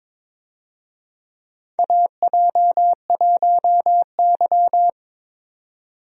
Morse code, heard as a keyed tone, about 22 wpm, 700 Hz. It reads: AJ1Y
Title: USA Call Signs - 4 Characters - 22wpm